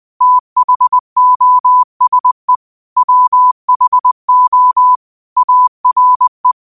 Morse code, heard as a keyed tone, about 20 wpm, 1000 Hz. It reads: THOSE WHO ARE
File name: qotdfiles / 20wpm